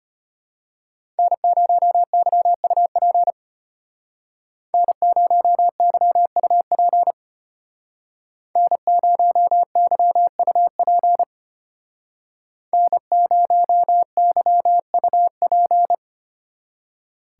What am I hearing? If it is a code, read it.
Morse code, heard as a keyed tone, about 31 wpm, 700 Hz. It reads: N0YUP N0YUP N0YUP N0YUP
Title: Speed Racing - USA 1x3 Call Signs 25wpm